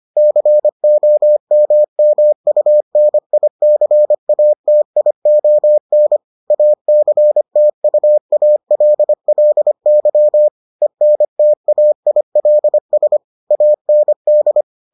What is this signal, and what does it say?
Morse code, heard as a keyed tone, about 25 wpm, 600 Hz. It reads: COMMUNICATION ACTUALLY ENTAILS AND